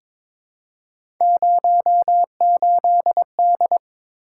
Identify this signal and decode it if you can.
Morse code, heard as a keyed tone, about 22 wpm, 700 Hz. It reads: 08D
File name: practice / 22wpm / Three Letter-Number Combinations - 22wpm